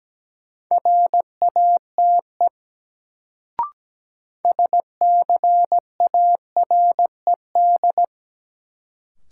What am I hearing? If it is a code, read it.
Morse code, heard as a keyed tone, about 17 wpm, 700 Hz. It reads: RATE SCARED
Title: Sentences from Top 700 Words - New Word Drill - 17wpm